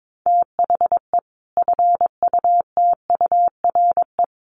Morse code, heard as a keyed tone, about 22 wpm, 700 Hz. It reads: THE FUTURE
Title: Sentences from Top 700 Words 22wpm